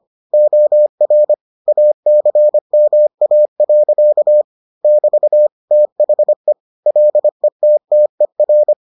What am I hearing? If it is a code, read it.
Morse code, heard as a keyed tone, about 25 wpm, 600 Hz. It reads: OR ACMA. <BT> THE LETTER